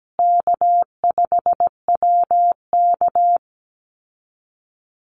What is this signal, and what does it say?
Morse code, heard as a keyed tone, about 17 wpm, 700 Hz. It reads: K5WK
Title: Easy Call Signs Repeated 3x - 17wpm